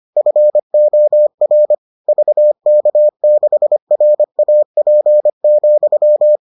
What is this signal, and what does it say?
Morse code, heard as a keyed tone, about 25 wpm, 600 Hz. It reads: FOR VK6RAP,